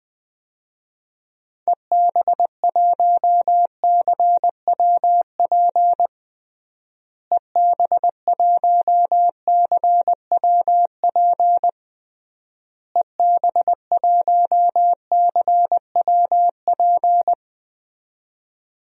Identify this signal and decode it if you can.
Morse code, heard as a keyed tone, about 20 wpm, 700 Hz. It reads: EB1CWP EB1CWP EB1CWP
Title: DXCC-100 Call Signs - 6 Characters - Repeated 3x - 20wpm